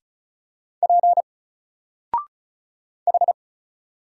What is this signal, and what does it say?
Morse code, heard as a keyed tone, about 36 wpm, 700 Hz. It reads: P H
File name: Single-Letter-Number-35wpm